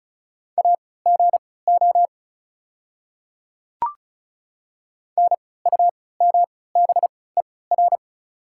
Morse code, heard as a keyed tone, about 34 wpm, spaced as for 20 wpm, 690 Hz. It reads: AGO NUMBER
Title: 3x Character Spacing - Top 500 Words - 35wpm